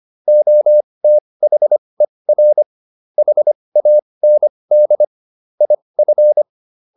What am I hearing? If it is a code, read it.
Morse code, heard as a keyed tone, about 25 wpm, spaced as for 20 wpm, 600 Hz. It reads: OTHER HAND IF